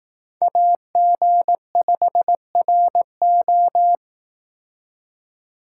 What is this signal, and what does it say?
Morse code, heard as a keyed tone, about 18 wpm, 700 Hz. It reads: AG5RO
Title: Speed Racing - USA Call Signs 15wpm